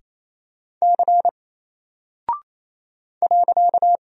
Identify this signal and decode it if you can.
Morse code, heard as a keyed tone, about 28 wpm, 700 Hz. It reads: C .
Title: Single Character (Letter:Number:Punctuation:Prosign) 28wpm